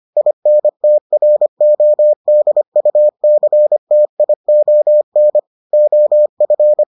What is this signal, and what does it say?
Morse code, heard as a keyed tone, about 25 wpm, 600 Hz. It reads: INTRODUCTION OF